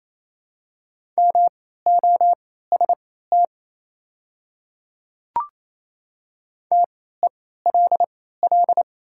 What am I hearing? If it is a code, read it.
Morse code, heard as a keyed tone, about 28 wpm, spaced as for 16 wpm, 700 Hz. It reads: MOST TELL
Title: 3x Character Spacing - Top 300 Words - 28wpm